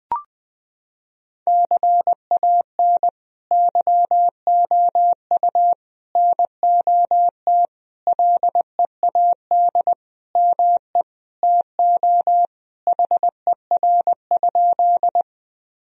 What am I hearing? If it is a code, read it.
Morse code, heard as a keyed tone, about 19 wpm, 700 Hz. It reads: CAN YOU NOT LEAD ME TO HER?
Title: Sentences from Top 1000 Words - Review Part 2 - 20wpm